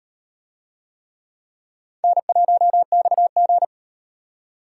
Morse code, heard as a keyed tone, about 36 wpm, 700 Hz. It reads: N1XG